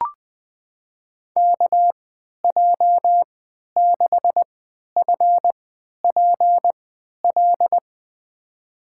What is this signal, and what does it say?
Morse code, heard as a keyed tone, about 20 wpm, spaced as for 11 wpm, 700 Hz. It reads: KJ6FPL
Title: USA Call Signs - 6 Characters - 3x Character Spacing - 20wpm